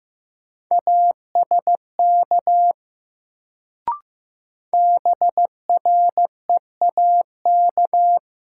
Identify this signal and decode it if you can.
Morse code, heard as a keyed tone, about 15 wpm, 700 Hz. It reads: ASK BREAK
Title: Sets of 4 Words - New Word Drill - 15wpm